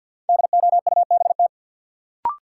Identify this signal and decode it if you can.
Morse code, heard as a keyed tone, about 50 wpm, 700 Hz. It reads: DOUBT
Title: Sets of 5 Words - New Word Drill - 50wpm